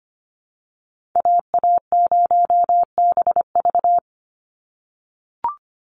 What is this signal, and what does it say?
Morse code, heard as a keyed tone, about 25 wpm, 700 Hz. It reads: AA0BV
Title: NA SOTA Activator Callsigns 25wpm